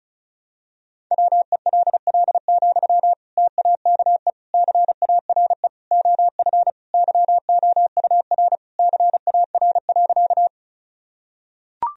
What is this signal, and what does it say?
Morse code, heard as a keyed tone, about 34 wpm, 700 Hz. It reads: WELL, TAKE CARE OF YOUR CAR.
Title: Sentences from Top 1000 Words - Review Part 1 - 35wpm